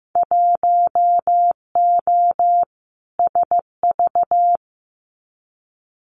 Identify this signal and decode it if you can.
Morse code, heard as a keyed tone, about 15 wpm, 700 Hz. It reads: 1O SV
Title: Winter Field Day Exchanges 15wpm